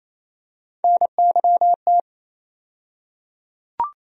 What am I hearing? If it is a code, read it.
Morse code, heard as a keyed tone, about 27 wpm, 700 Hz. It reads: NYT
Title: USA Call Sign Postfixes - 3 Letters 28wpm